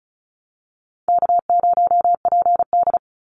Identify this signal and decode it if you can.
Morse code, heard as a keyed tone, about 34 wpm, 700 Hz. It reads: K0PD
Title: Easy Call Signs Repeated 3x - 35wpm